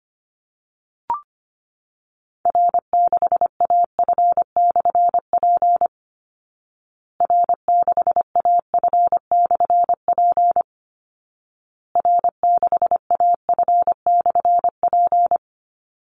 Morse code, heard as a keyed tone, about 25 wpm, 700 Hz. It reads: R6AF/P R6AF/P R6AF/P
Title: Call Signs Repeated 2x - 25wpm